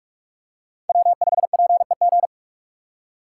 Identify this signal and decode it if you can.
Morse code, heard as a keyed tone, about 46 wpm, 700 Hz. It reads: W5PEG
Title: USA Call Signs - 3x Character Spacing - 45wpm